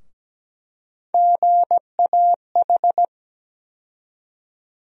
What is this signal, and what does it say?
Morse code, heard as a keyed tone, about 17 wpm, 700 Hz. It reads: GAH
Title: USA Call Sign Postfixes - 3 Letters 17wpm